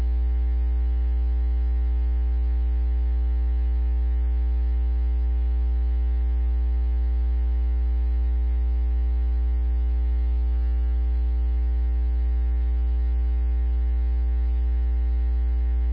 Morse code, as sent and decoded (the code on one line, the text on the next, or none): none